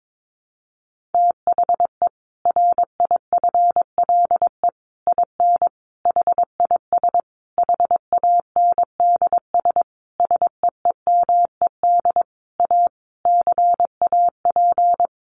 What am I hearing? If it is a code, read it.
Morse code, heard as a keyed tone, about 22 wpm, 700 Hz. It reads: THE RIFLE IN HIS HANDS SEEMED A CAP
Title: A Sound Of Thunder 22wpm